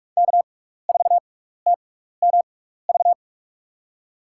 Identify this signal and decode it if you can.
Morse code, heard as a keyed tone, about 44 wpm, spaced as for 15 wpm, 700 Hz. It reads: K4TMV
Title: USA Call Signs - 5 Characters - 6x Character Spacing - 45wpm